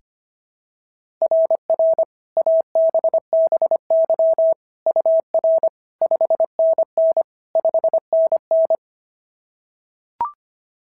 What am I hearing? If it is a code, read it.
Morse code, heard as a keyed tone, about 25 wpm, 690 Hz. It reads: RR ABBY UR 5NN 5NN
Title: SOTA QSO Practice 25wpm